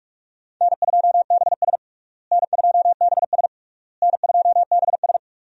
Morse code, heard as a keyed tone, about 42 wpm, 700 Hz. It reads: N2BS N2BS N2BS
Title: USA Call Signs - 4 Characters - Repeated 3x - 45wpm